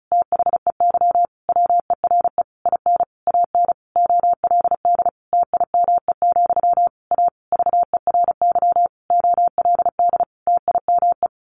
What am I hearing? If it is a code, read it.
Morse code, heard as a keyed tone, about 34 wpm, 700 Hz. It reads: THEY WERE IN AN OLD TIME, A VERY OLD TIME
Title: A Sound Of Thunder 35wpm